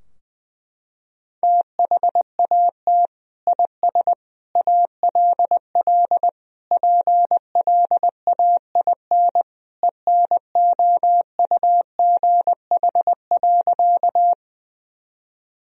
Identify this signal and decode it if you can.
Morse code, heard as a keyed tone, about 20 wpm, 700 Hz. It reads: THAT IS ALL PLAIN ENOUGH.
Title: Sentences from Top 1000 Words - Review Part 2 - 20wpm